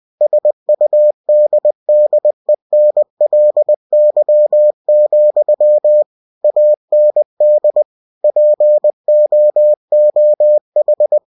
Morse code, heard as a keyed tone, about 20 wpm, 600 Hz. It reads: SUDDENLY, AND POOH
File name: cw-books / winnie-the-pooh / winnie-20wpm-page-0006